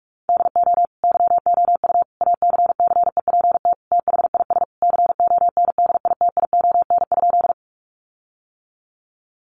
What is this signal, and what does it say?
Morse code, heard as a keyed tone, about 44 wpm, 700 Hz. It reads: DO YOU ACCEPT THIS CONDITION?
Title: Sentences from Top 1300 Words 45wpm